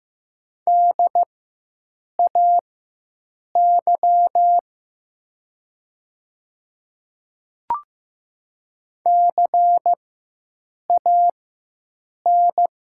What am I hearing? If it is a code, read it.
Morse code, heard as a keyed tone, about 15 wpm, spaced as for 7 wpm, 700 Hz. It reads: DAY CAN
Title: 4x Character Spacing - Top 100 Words - 15wpm